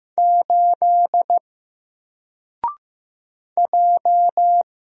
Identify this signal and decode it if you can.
Morse code, heard as a keyed tone, about 15 wpm, 700 Hz. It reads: 8 J
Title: Single Letter-Number Alt 15wpm